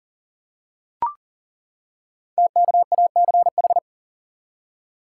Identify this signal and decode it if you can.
Morse code, heard as a keyed tone, about 38 wpm, 690 Hz. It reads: TKACH
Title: Code Groups - 5 Letters - 40wpm